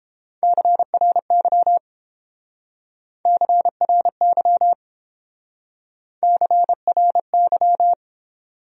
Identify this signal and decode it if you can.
Morse code, heard as a keyed tone, about 31 wpm, 700 Hz. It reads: CRY CRY CRY